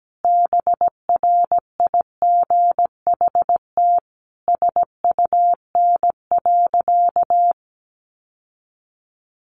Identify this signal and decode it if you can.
Morse code, heard as a keyed tone, about 17 wpm, 700 Hz. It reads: BRIGHT SUN.
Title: Sentences from Top 1000 Words 17wpm